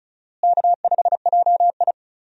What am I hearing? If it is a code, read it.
Morse code, heard as a keyed tone, about 36 wpm, 700 Hz. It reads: K5JI
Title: USA Call Signs - 4 Characters - Repeated 3x - 35wpm